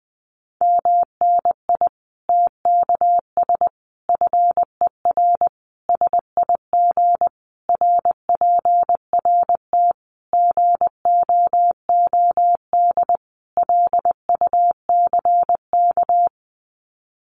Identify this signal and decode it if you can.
Morse code, heard as a keyed tone, about 20 wpm, 700 Hz. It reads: MNI TKS FER SIG RPRT GOOD LUCK
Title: Common QSO Phrases 20wpm